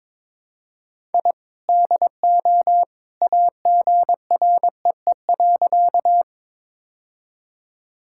Morse code, heard as a keyed tone, about 22 wpm, 700 Hz. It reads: I DO AGREE.
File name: practice / 22wpm / Sentences with 2-Word Phrases - 22wpm